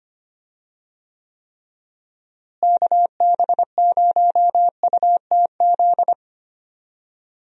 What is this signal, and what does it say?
Morse code, heard as a keyed tone, about 25 wpm, 700 Hz. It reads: KB0UTZ
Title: USA Call Signs - 6 Characters - Repeated 3x - 25wpm